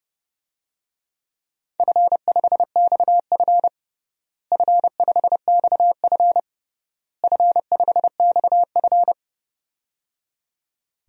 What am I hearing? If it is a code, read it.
Morse code, heard as a keyed tone, about 30 wpm, 700 Hz. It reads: F5XF F5XF F5XF
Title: DXCC-100 Call Signs - 4 Characters - Repeated 3x - 30wpm